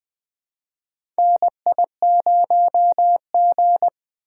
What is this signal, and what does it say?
Morse code, heard as a keyed tone, about 20 wpm, 700 Hz. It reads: NI0G
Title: USA Call Signs - 4 Characters - 3x Character Spacing - 20wpm